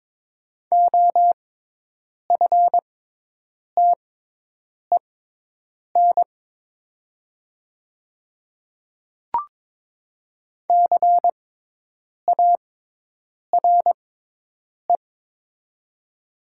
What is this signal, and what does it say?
Morse code, heard as a keyed tone, about 22 wpm, spaced as for 8 wpm, 700 Hz. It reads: OFTEN CARE
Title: 6x Character Spacing - Top 300 Words - 22wpm